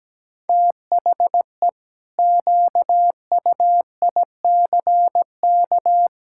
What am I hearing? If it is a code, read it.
Morse code, heard as a keyed tone, about 17 wpm, 700 Hz. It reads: THE QUICK
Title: Pangram Challenge v2 - 17wpm